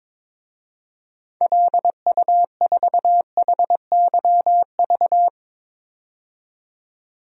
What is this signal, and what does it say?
Morse code, heard as a keyed tone, about 22 wpm, 700 Hz. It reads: LU4HYV
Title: DXCC-100 Call Signs - 6 Characters - 22wpm